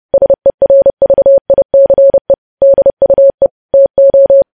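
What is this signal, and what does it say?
Morse code, heard as a keyed tone, about 30 wpm, 550 Hz. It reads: SERVICE DUE TO
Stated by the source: Morse code news headlines sent at 30, preceded by vvv vvv.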